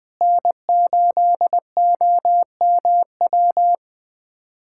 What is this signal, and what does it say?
Morse code, heard as a keyed tone, about 20 wpm, 700 Hz. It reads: N8OMW
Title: USA Callsign DE Callsign - 5 Characters - 20wpm